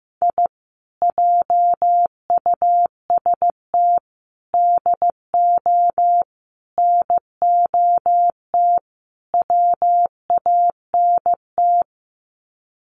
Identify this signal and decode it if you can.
Morse code, heard as a keyed tone, about 15 wpm, 700 Hz. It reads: I JUST DO NOT WANT
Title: Sets of 5 Words Encore2 15wpm